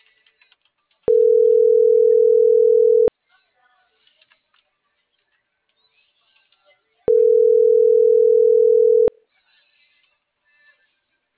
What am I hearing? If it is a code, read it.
Ringback tone (North America), 440 + 480 Hz.